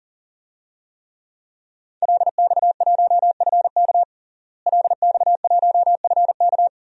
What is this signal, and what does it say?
Morse code, heard as a keyed tone, about 40 wpm, 700 Hz. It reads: LX1FK LX1FK
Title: DXCC-100 Call Signs - 5 Characters - Repeated 2x - 40wpm